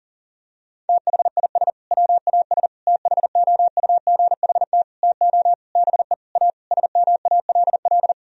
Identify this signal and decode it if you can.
Morse code, heard as a keyed tone, about 40 wpm, 700 Hz. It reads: THIS WAS THOUGHT TO BE A SMALL
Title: Sentences from Top 1000 Words - Review Part 1 - 40wpm